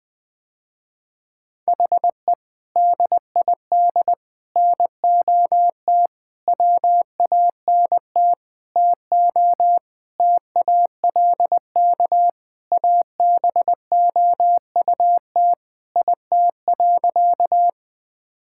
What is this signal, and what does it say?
Morse code, heard as a keyed tone, about 21 wpm, 700 Hz. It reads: HE DID NOT WANT TO TALK ABOUT IT.
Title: Sentences with 5-Word Phrases - Spoken Beginning - 20wpm